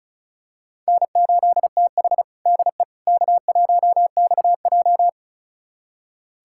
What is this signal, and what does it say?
Morse code, heard as a keyed tone, about 36 wpm, 700 Hz. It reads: N8TH DE K1XJ
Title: USA Callsign DE Callsign - 4 Characters - 35wpm